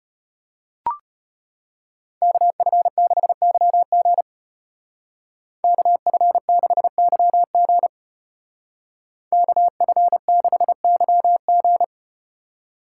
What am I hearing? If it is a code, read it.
Morse code, heard as a keyed tone, about 34 wpm, 700 Hz. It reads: KF6YG KF6YG KF6YG